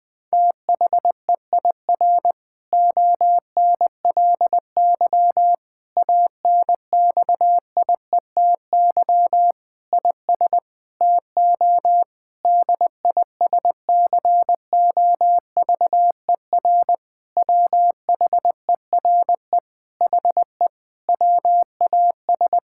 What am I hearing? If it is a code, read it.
Morse code, heard as a keyed tone, about 19 wpm, 700 Hz. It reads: THEIR ONLY ANXIETY IS TO DISCOVER WHERE HE WAS
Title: Sentences from Top 1600 Words 20wpm